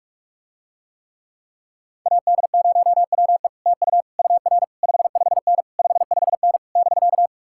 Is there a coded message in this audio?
Yes: Morse code from 2.1 to 7.3 s, heard as a keyed tone, about 47 wpm, 700 Hz. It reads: AD0WE TU UR 55N 55N <BK>